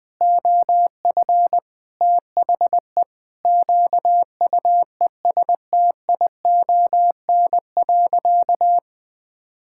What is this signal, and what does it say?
Morse code, heard as a keyed tone, about 21 wpm, 700 Hz. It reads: OF THE QUESTION.
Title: Sentences from Top 1000 Words - Review Part 5 - 20wpm